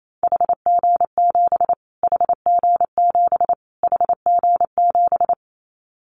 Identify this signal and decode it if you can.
Morse code, heard as a keyed tone, about 28 wpm, 700 Hz. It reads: HG7 HG7 HG7
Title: Callsign Prefixes for Top 25 DXCC - Repeated 3x - 28wpm